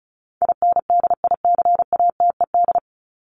Morse code, heard as a keyed tone, about 34 wpm, 700 Hz. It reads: INDICATED
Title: ED Suffix - Top 100 Words with Base Words 35wpm